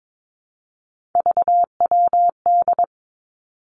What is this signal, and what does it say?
Morse code, heard as a keyed tone, about 22 wpm, 700 Hz. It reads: VWD
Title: Three Challenging Letter-Number Combinations 22wpm